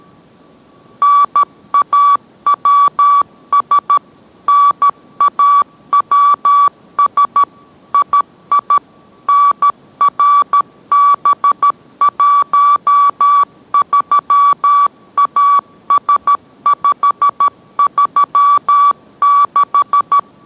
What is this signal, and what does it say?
Morse code, heard as a keyed tone, about 13 wpm, 1150 Hz. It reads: NAWS NAWS II NRB13AS536